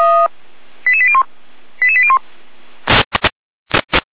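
Five-tone selective call (ZVEI), heard as quick runs of five tones, tones of about 70 ms. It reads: 88801 88801
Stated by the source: ZVEI five-tone selective call 88537 received on SDR.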